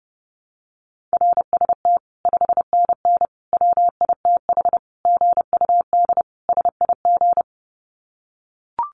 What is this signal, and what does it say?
Morse code, heard as a keyed tone, about 30 wpm, 700 Hz. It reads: RST 5NN WITH GUD SIG
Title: Common QSO Phrases 30wpm